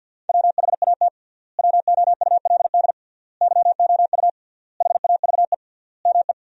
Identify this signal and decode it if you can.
Morse code, heard as a keyed tone, about 53 wpm, 700 Hz. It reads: WHAT WOULD YOU HAVE ME
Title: 3x Word-Spacing - Sentences from Top 300 Words - 50wpm